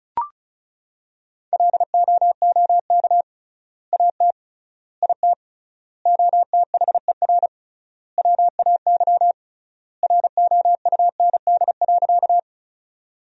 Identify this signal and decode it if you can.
Morse code, heard as a keyed tone, about 36 wpm, 700 Hz. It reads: LOOK AT IT OTHER WAY ROUND.
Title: 3x Word-Spacing - Sentences from Top 200 Words - 35wpm